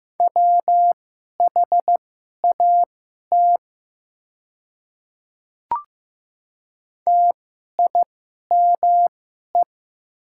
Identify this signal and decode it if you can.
Morse code, heard as a keyed tone, about 15 wpm, spaced as for 11 wpm, 690 Hz. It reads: WHAT TIME
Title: 2x Character Spacing - Top 100 Words - 15wpm